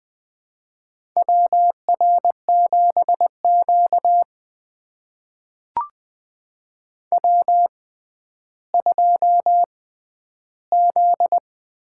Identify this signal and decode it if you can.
Morse code, heard as a keyed tone, about 20 wpm, spaced as for 11 wpm, 700 Hz. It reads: WR7Q W2Z
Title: USA Call Signs - 4 Characters - 6x Character Spacing - 20wpm